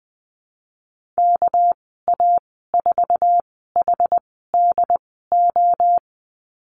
Morse code, heard as a keyed tone, about 20 wpm, spaced as for 15 wpm, 700 Hz. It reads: KA4HDO
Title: 231 Lesson 37g - Letter x - Call Signs 20wpm